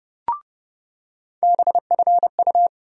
Morse code, heard as a keyed tone, about 30 wpm, 700 Hz. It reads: BFU
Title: USA Call Sign Postfixes - 3 Letters 30wpm